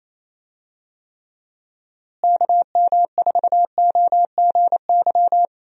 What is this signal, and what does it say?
Morse code, heard as a keyed tone, about 27 wpm, 700 Hz. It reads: KM4OGY
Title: USA Call Signs - 6 Characters - Repeated 3x - 28wpm